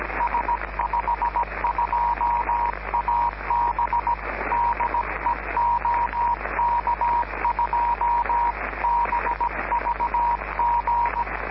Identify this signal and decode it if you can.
Morse code, heard as a keyed tone, about 17 wpm, 950 Hz, with noise. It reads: S52AB DE OK2BVG